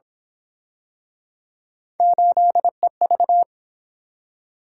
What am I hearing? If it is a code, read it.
Morse code, heard as a keyed tone, about 28 wpm, 700 Hz. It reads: 8EV